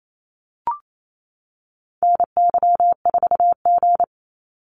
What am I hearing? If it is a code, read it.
Morse code, heard as a keyed tone, about 28 wpm, 700 Hz. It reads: NY4G
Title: NA SOTA Activator Callsigns 28wpm